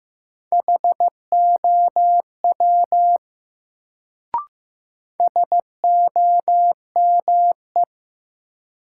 Morse code, heard as a keyed tone, about 15 wpm, 700 Hz. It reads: HOW SOME
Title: Sentences from Top 100 Words - New Word Drill - 15wpm